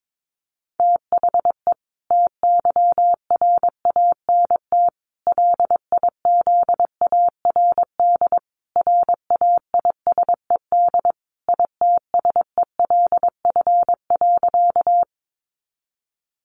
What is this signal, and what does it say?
Morse code, heard as a keyed tone, about 22 wpm, 700 Hz. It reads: THE TYRANT LIZARD RAISED ITSELF.
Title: A Sound Of Thunder 22wpm